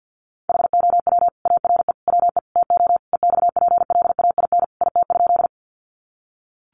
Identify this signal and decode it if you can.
Morse code, heard as a keyed tone, about 50 wpm, 700 Hz. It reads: HOW ARE WE TO EXPLAIN IT?
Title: Sentences from Top 1100 Words 50wpm